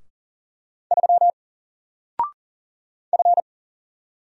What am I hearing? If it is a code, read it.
Morse code, heard as a keyed tone, about 40 wpm, 700 Hz. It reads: 3 F
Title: Single Letter-Number 40wpm